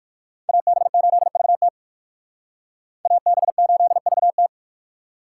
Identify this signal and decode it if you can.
Morse code, heard as a keyed tone, about 56 wpm, 700 Hz. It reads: AB8VT AB8VT
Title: Speed Racing - USA 2x2 Call Signs 45wpm